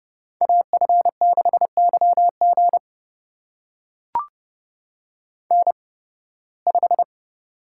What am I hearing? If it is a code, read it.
Morse code, heard as a keyed tone, about 30 wpm, spaced as for 19 wpm, 700 Hz. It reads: AF6YG N5
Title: USA Call Signs - 5 Characters - 8x Character Spacing - 30wpm